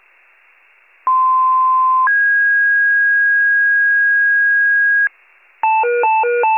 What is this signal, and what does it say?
Two-tone page, heard as a one-second tone, then a three-second tone, about 1040 then 1650 Hz. It is heard from 1.1 to 5.1 s.
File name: pager / 95 RESCUE DUTY / wcalert